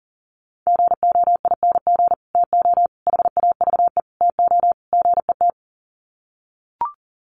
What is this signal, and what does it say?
Morse code, heard as a keyed tone, about 40 wpm, 700 Hz. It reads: GOING TO HAVE TO GET
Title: Sets of 5 Words Encore2 40wpm